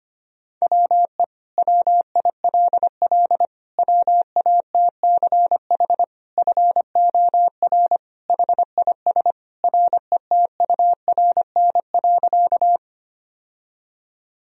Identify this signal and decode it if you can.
Morse code, heard as a keyed tone, about 25 wpm, 700 Hz. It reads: WE WILL WATCH FOR HIS RETURN.
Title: Sentences from Top 1000 Words - Review Part 1 - 25wpm